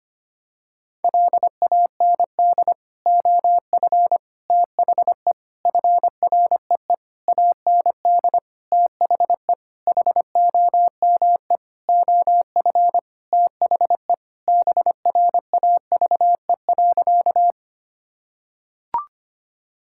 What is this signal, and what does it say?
Morse code, heard as a keyed tone, about 25 wpm, 700 Hz. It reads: LAND OF THE FREE AND THE HOME OF THE BRAVE.
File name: practice / 25wpm / Sentences with 9-Word Phrases - 25wpm